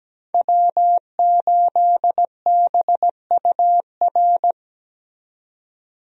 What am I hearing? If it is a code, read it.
Morse code, heard as a keyed tone, about 17 wpm, 700 Hz. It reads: W8BUR
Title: USA Call Signs - 5 Characters - 17wpm